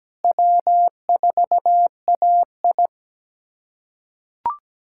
Morse code, heard as a keyed tone, about 17 wpm, 700 Hz. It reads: W4AI
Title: USA Callsign DE Callsign - 4 Characters - 17wpm